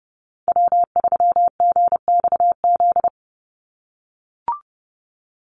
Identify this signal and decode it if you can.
Morse code, heard as a keyed tone, about 30 wpm, 700 Hz. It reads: W3GXZ
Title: USA Call Signs 30wpm